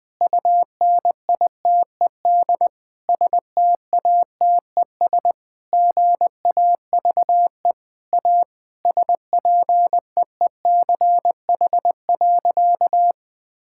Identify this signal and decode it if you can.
Morse code, heard as a keyed tone, about 19 wpm, 700 Hz. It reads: UNITED STATES GAVE A SPEECH.
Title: Sentences with 5-Word Phrases - 20wpm